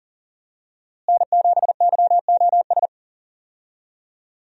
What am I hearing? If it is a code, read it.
Morse code, heard as a keyed tone, about 40 wpm, 700 Hz. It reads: N7YOS
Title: USA Call Signs - 5 Characters - Repeated 2x - 40wpm